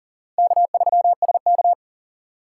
Morse code, heard as a keyed tone, about 37 wpm, 700 Hz. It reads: K3SK
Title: USA Call Signs - 4 Characters - 2x Character Spacing - 40wpm